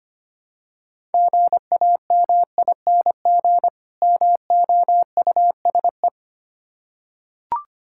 Morse code, heard as a keyed tone, about 25 wpm, 700 Hz. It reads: GAMING MOUSE
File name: practice / 25wpm / Difficult Words with Hints - 25wpm